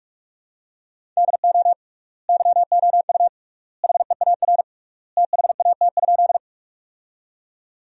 Morse code, heard as a keyed tone, about 47 wpm, 700 Hz. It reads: DO YOU HEAR THAT?
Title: 3x Word-Spacing - Sentences from Top 300 Words - 45wpm